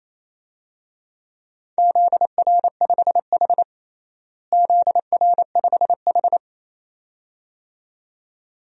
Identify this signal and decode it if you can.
Morse code, heard as a keyed tone, about 28 wpm, 700 Hz. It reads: ZR5H ZR5H
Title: DXCC-100 Call Signs - 4 Characters - Repeated 2x - 28wpm